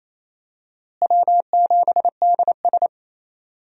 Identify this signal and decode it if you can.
Morse code, heard as a keyed tone, about 28 wpm, 700 Hz. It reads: W7DS
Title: USA Call Signs - 4 Characters - Repeated 3x - 28wpm